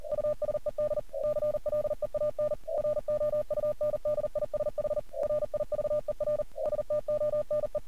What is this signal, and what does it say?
Morse code, heard as a keyed tone, about 37 wpm, 600 Hz. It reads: USED CLEAN ROUNDISH RIVER STONE
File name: archived / CRSnet